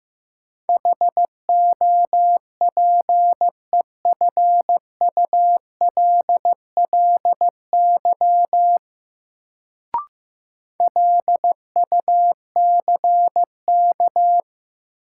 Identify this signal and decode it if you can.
Morse code, heard as a keyed tone, about 16 wpm, 700 Hz. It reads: HOPEFULLY LUCK
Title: Sentences from Top 1500 Words - New Word Drill - 15wpm